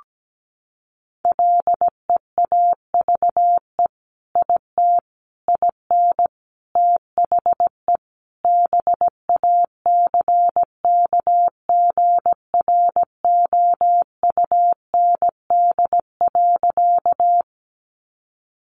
Morse code, heard as a keyed tone, about 17 wpm, 700 Hz. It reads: LEAVE IT IN THE BACKGROUND.
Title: Sentences from Top 1300 Words 17wpm